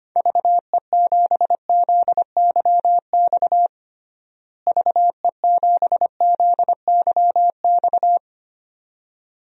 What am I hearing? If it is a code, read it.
Morse code, heard as a keyed tone, about 25 wpm, 700 Hz. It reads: VE7ZYX VE7ZYX
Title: DXCC-100 Call Signs - 6 Characters - Repeated 2x - 25wpm